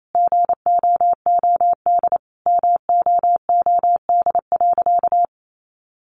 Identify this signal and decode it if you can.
Morse code, heard as a keyed tone, about 28 wpm, 700 Hz. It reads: GOOD MOOD.
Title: Sentences from Top 1100 Words 28wpm